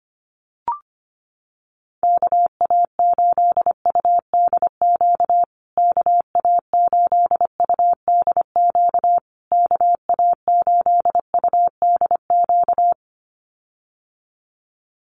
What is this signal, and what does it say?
Morse code, heard as a keyed tone, about 25 wpm, 700 Hz. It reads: KA8UDQ KA8UDQ KA8UDQ
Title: USA Call Signs Repeated 3x 25wpm